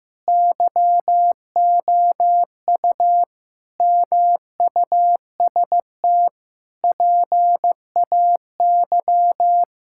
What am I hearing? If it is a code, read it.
Morse code, heard as a keyed tone, about 16 wpm, 700 Hz. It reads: YOU MUST PAY
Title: Sentences from Top 1000 Words - Review Part 5 - 15wpm